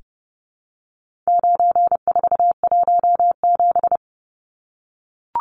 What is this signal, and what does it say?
Morse code, heard as a keyed tone, about 30 wpm, 700 Hz. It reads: 9417
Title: Numbers - 4-Digit 30wpm